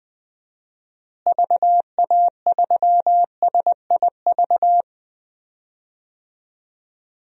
Morse code, heard as a keyed tone, about 20 wpm, 700 Hz. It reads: VA3SIV
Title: DXCC-100 Call Signs - 6 Characters - Repeated 2x - 20wpm